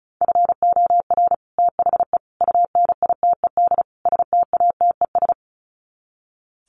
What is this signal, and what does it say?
Morse code, heard as a keyed tone, about 34 wpm, 700 Hz. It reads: FOR THE UNITED STATES
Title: Sets of 4 Words 35wpm